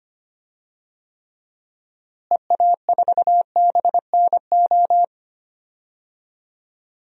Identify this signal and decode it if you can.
Morse code, heard as a keyed tone, about 25 wpm, 700 Hz. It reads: EA4BNO
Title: DXCC-100 Call Signs - 6 Characters - Repeated 2x - 25wpm